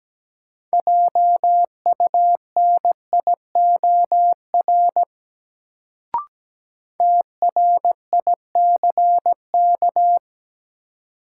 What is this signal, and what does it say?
Morse code, heard as a keyed tone, about 17 wpm, 700 Hz. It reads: JUNIOR TRICK
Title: Sentences from Top 1500 Words - New Word Drill Alt - 17wpm